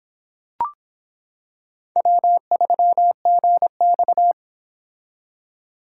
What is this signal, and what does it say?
Morse code, heard as a keyed tone, about 26 wpm, 700 Hz. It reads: W3GX